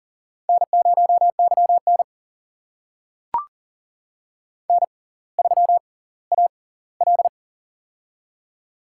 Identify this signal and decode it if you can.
Morse code, heard as a keyed tone, about 39 wpm, spaced as for 20 wpm, 700 Hz. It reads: N0YN N3AL
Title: USA Call Signs - 4 Characters - 6x Character Spacing - 40wpm